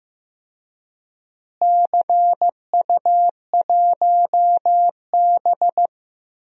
Morse code, heard as a keyed tone, about 15 wpm, 700 Hz. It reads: CU1B
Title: DXCC-100 Call Signs - 4 Characters - Repeated 3x - 15wpm